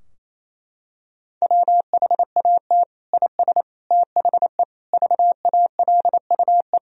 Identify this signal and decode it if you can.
Morse code, heard as a keyed tone, about 28 wpm, 700 Hz. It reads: WHAT IS THE VALUE
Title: Sentences from Top 1000 Words - Review Part 3 - 28wpm